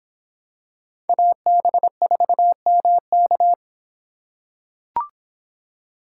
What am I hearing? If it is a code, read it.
Morse code, heard as a keyed tone, about 26 wpm, 700 Hz. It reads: AB4MK